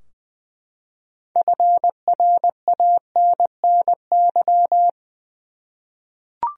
Morse code, heard as a keyed tone, about 20 wpm, 700 Hz. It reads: FRANNY
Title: USA Common Names - 20wpm